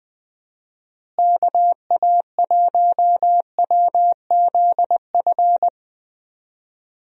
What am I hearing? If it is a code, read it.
Morse code, heard as a keyed tone, about 21 wpm, 700 Hz. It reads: KA1WZF
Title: USA Call Signs - 6 Characters - Repeated 3x - 20wpm